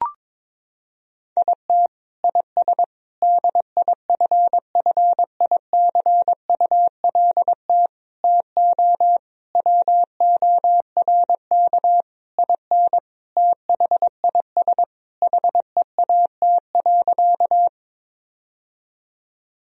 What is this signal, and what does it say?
Morse code, heard as a keyed tone, about 22 wpm, 700 Hz. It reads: IT IS DIFFICULT TO WORK IN THIS HEAT.
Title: Sentences from Top 1000 Words - Review Part 3 - 22wpm